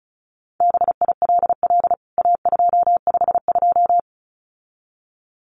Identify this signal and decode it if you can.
Morse code, heard as a keyed tone, about 34 wpm, 700 Hz. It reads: BILL A252
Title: CWT Exchange 35wpm